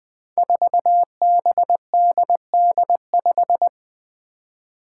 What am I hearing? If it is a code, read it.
Morse code, heard as a keyed tone, about 20 wpm, 700 Hz. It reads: 4BDD5
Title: Five Challenging Letter-Number Combinations 20wpm